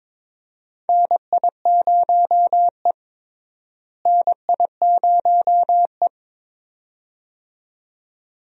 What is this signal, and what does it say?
Morse code, heard as a keyed tone, about 22 wpm, 700 Hz. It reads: NI0E NI0E
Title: USA Call Signs - 4 Characters - Repeated 2x - 22wpm